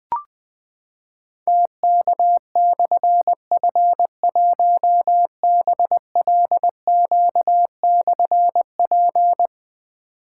Morse code, heard as a keyed tone, about 20 wpm, 700 Hz. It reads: TK/F1BLQ/P
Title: Difficult Call Signs 20wpm